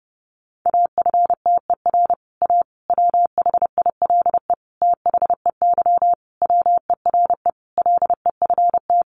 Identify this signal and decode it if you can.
Morse code, heard as a keyed tone, about 30 wpm, 700 Hz. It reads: AFTER A WHILE THEY WERE LEFT
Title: Sentences from Top 1000 Words 30wpm